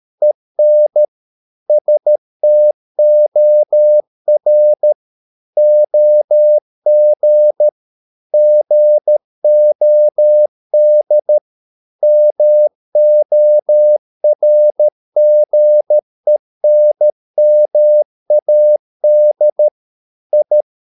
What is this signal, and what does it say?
Morse code, heard as a keyed tone, about 13 wpm, 600 Hz. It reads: EN STOR OG GOD MORGENMAD I
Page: Ferie 13 wpm